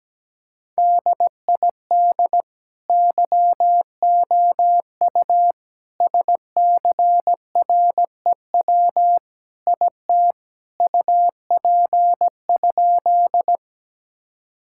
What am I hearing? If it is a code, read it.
Morse code, heard as a keyed tone, about 17 wpm, 700 Hz. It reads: DID YOU SCREW IT UP?
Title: Sentences from Top 1600 Words 17wpm